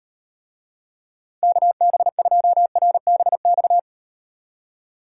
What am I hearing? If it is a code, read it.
Morse code, heard as a keyed tone, about 38 wpm, 700 Hz. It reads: KB2RBX